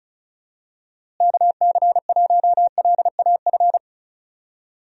Morse code, heard as a keyed tone, about 36 wpm, 700 Hz. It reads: KC1LAF
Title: USA Call Signs - 6 Characters - Repeated 3x - 35wpm